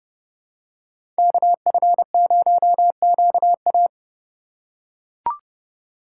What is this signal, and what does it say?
Morse code, heard as a keyed tone, about 31 wpm, 700 Hz. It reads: KF0QA
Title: USA Call Signs - 5 Characters - 4x Character Spacing - 30wpm